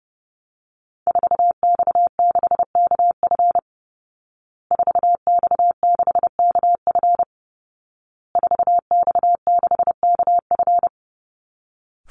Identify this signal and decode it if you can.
Morse code, heard as a keyed tone, about 28 wpm, 700 Hz. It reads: 4X6KF 4X6KF 4X6KF
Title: Call Signs Repeated 2x - 30wpm